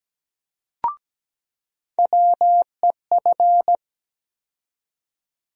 Morse code, heard as a keyed tone, about 17 wpm, 700 Hz. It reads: WEF
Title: USA Call Sign Postfixes - 3 Letters 17wpm